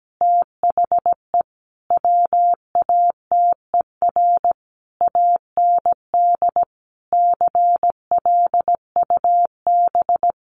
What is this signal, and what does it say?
Morse code, heard as a keyed tone, about 17 wpm, 700 Hz. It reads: THE WATER AND CLUB
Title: Sentences from Top 1200 Words 17wpm